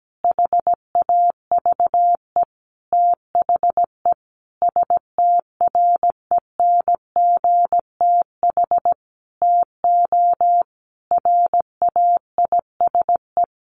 Morse code, heard as a keyed tone, about 17 wpm, 700 Hz. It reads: HAVE THE STRENGTH TO RAISE
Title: Sentences from Top 1400 Words 17wpm